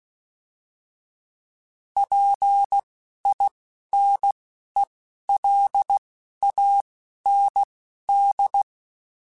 Morse code, heard as a keyed tone, about 16 wpm, spaced as for 11 wpm, 800 Hz. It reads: PINELAND